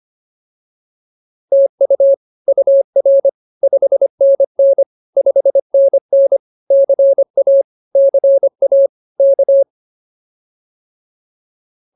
Morse code, heard as a keyed tone, about 24 wpm, 550 Hz. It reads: TU UR 5NN 5NN CA CA K